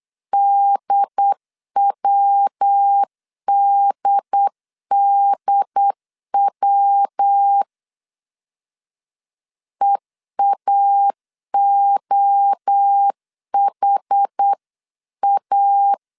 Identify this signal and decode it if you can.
Morse code, heard as a keyed tone, about 8 wpm, 800 Hz. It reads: DWDDW EAOHA